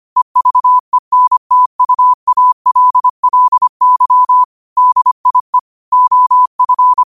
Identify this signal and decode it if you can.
Morse code, heard as a keyed tone, about 25 wpm, 1000 Hz. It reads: EVENTUALLY DIE OF